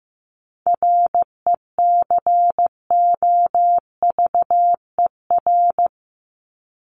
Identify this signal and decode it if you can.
Morse code, heard as a keyed tone, about 15 wpm, 700 Hz. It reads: RECOVER
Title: Top 1751-2000 Words 15wpm